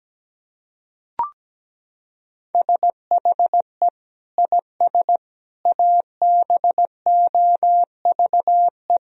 Morse code, heard as a keyed tone, about 17 wpm, 700 Hz. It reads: SHE IS ABOVE
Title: Sentences from Top 1000 Words - Review Part 2 - 17wpm